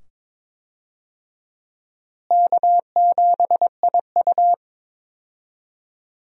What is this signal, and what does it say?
Morse code, heard as a keyed tone, about 22 wpm, 700 Hz. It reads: K7IU
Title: USA Call Signs - 4 Characters - Repeated 2x - 22wpm